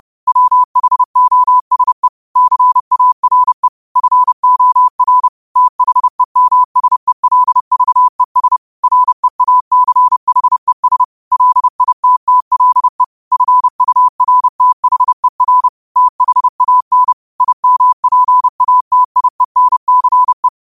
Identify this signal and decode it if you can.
Morse code, heard as a keyed tone, about 30 wpm, 1000 Hz. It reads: WHOSE CARE FOR THEMSELVES REACHES LITTLE FURTHER THAN IMPATIENCE
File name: qotdfiles / 30wpm